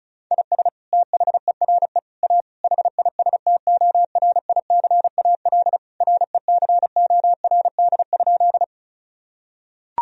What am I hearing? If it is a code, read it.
Morse code, heard as a keyed tone, about 36 wpm, 700 Hz. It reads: IS THERE A HISTORICAL RECORD?
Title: Sentences from Top 1000 Words - Review Part 4 - 35wpm